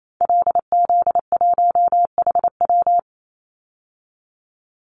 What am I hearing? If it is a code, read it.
Morse code, heard as a keyed tone, about 28 wpm, 700 Hz. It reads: LZ1HW
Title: Call Signs Repeated 2x - 28wpm